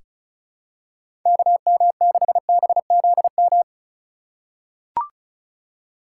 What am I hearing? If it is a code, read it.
Morse code, heard as a keyed tone, about 34 wpm, 700 Hz. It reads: KM6BZM
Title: USA Call Signs - 6 Characters - 3x Character Spacing - 35wpm